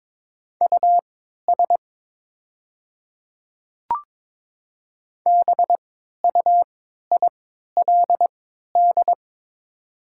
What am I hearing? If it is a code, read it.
Morse code, heard as a keyed tone, about 22 wpm, spaced as for 12 wpm, 700 Hz. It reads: US BUILD
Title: 3x Character Spacing - Top 300 Words - 22wpm